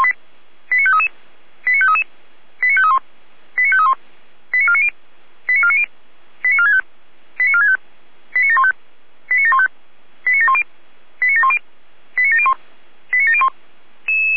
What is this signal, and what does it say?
Five-tone selective call (ZVEI), heard as quick runs of five tones, tones of about 70 ms. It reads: 89533 89533 89531 89531 89490 89490 89465 89465 89715 89715 89710 89710 89701 89701
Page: /455/SDR/ZVEI/